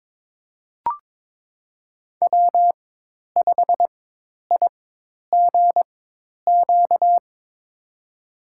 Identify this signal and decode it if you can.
Morse code, heard as a keyed tone, about 22 wpm, spaced as for 10 wpm, 700 Hz. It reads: W5IGQ
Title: USA Call Signs - 5 Characters - 4x Character Spacing - 22wpm